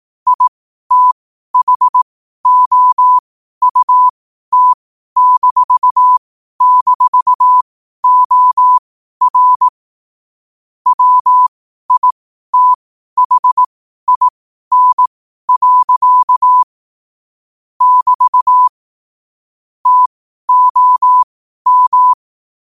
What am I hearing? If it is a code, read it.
Morse code, heard as a keyed tone, about 18 wpm, spaced as for 12 wpm, 1000 Hz. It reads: ITHOUT--OR WITHIN. <BT> TOM
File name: qotdfiles / 12wpm